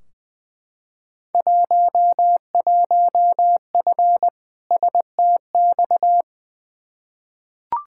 Morse code, heard as a keyed tone, about 20 wpm, 700 Hz. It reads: 11F STX
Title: Field Day Exchanges Encore - 20wpm